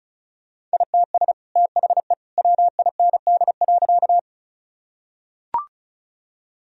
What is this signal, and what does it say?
Morse code, heard as a keyed tone, about 36 wpm, 700 Hz. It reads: ITS THE WIND.
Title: Sentences from Top 1000 Words - Review Part 2 - 35wpm